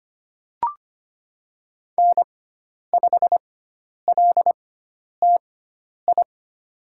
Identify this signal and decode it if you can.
Morse code, heard as a keyed tone, about 25 wpm, spaced as for 10 wpm, 700 Hz. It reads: N5LTI
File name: Lesson 16g - Letter c - Call Signs 25wpm